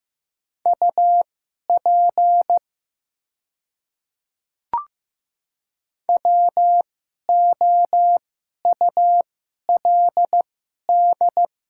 Morse code, heard as a keyed tone, about 15 wpm, spaced as for 11 wpm, 700 Hz. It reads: UP WOULD
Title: 2x Character Spacing - Top 100 Words - 15wpm